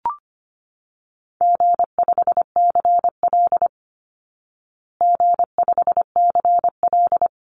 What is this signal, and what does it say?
Morse code, heard as a keyed tone, about 25 wpm, 700 Hz. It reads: G5CL G5CL
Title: Easy Call Signs Repeated 2x - 25wpm